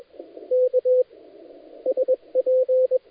Morse code, heard as a keyed tone, about 21 wpm, 510 Hz. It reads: K SP
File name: dlazaawansowanych / nagraniazpasma / titawaNETcontrol